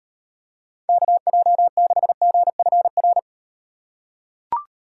Morse code, heard as a keyed tone, about 38 wpm, 700 Hz. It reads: KJ6GFR